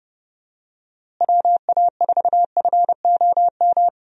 Morse code, heard as a keyed tone, about 31 wpm, 700 Hz. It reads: WA4FOM
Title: USA Call Signs - 6 Characters - Repeated 3x - 30wpm